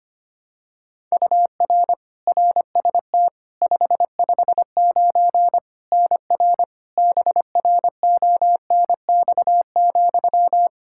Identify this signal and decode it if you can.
Morse code, heard as a keyed tone, about 25 wpm, 700 Hz. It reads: UR RST 559 NR BRONX,
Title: K1USN SST QSO Practice 25wpm